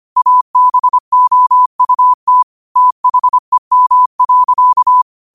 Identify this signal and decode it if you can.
Morse code, heard as a keyed tone, about 25 wpm, 1000 Hz. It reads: ABOUT THEM.
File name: qotdfiles / 25wpm